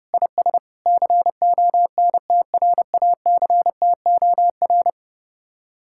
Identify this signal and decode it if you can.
Morse code, heard as a keyed tone, about 30 wpm, 700 Hz. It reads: IS CONTRACTOR